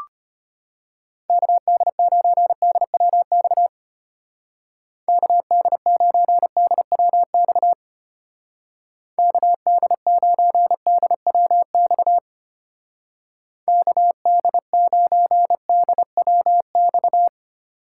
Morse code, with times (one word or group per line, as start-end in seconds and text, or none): 1.3-3.7 KD9DWX
5.1-7.7 KD9DWX
9.2-12.2 KD9DWX
13.7-17.3 KD9DWX